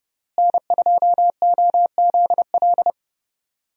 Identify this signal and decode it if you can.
Morse code, heard as a keyed tone, about 32 wpm, 700 Hz. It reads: N2OZL
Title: USA Call Signs - 5 Characters - Repeated 3x - 30wpm